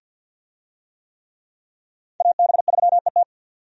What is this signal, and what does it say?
Morse code, heard as a keyed tone, about 47 wpm, 700 Hz. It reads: AB3ET
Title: USA Call Signs - 5 Characters - 50wpm